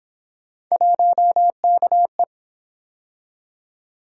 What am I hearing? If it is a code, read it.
Morse code, heard as a keyed tone, about 28 wpm, 700 Hz. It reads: 1KE